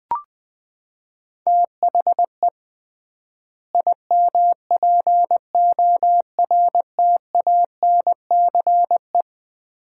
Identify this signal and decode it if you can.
Morse code, heard as a keyed tone, about 20 wpm, 700 Hz. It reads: THE IMPORTANCE
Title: 3x Word-Spacing - Sets of 3 Words - 20wpm